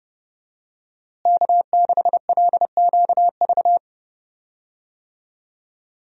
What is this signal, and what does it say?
Morse code, heard as a keyed tone, about 30 wpm, 700 Hz. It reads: K6LQV
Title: USA Call Signs - 5 Characters - Repeated 2x - 30wpm